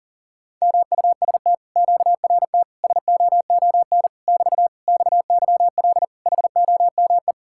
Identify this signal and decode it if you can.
Morse code, heard as a keyed tone, about 48 wpm, 700 Hz. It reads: MUST QRT SOON <BT> XYL HOME
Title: Speed-Racing Common QSO Phrases - 40wpm